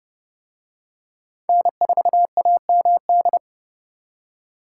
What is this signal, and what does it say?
Morse code, heard as a keyed tone, about 29 wpm, 700 Hz. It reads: N4AMD